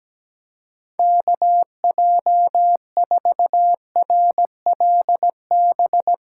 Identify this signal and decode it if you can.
Morse code, heard as a keyed tone, about 17 wpm, 700 Hz. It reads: KJ4RLB
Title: USA Callsign DE Callsign - 6 Characters - 17wpm